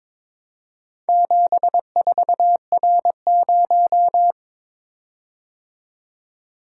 Morse code, heard as a keyed tone, about 23 wpm, 700 Hz. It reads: 74R0
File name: practice / 22wpm / Four Letter-Number Combinations - 22wpm